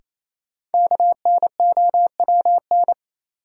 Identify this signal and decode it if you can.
Morse code, heard as a keyed tone, about 28 wpm, 700 Hz. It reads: KNOWN
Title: Sentences from Top 600 Words - New Word Drill - 28wpm